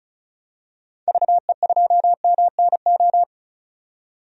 Morse code, heard as a keyed tone, about 36 wpm, 700 Hz. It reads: VE2MNO
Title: DXCC-100 Call Signs - 6 Characters - 35wpm